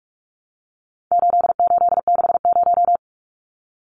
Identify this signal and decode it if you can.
Morse code, heard as a keyed tone, about 44 wpm, 700 Hz. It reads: 8860
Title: Numbers - 4-Digit 45wpm